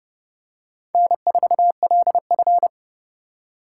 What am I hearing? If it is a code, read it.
Morse code, heard as a keyed tone, about 26 wpm, 700 Hz. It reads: N4LF